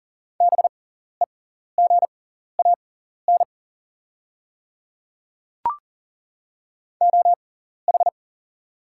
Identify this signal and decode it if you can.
Morse code, heard as a keyed tone, about 40 wpm, spaced as for 14 wpm, 690 Hz. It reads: BEGAN OH
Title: 6x Character Spacing - Top 500 Words - 40wpm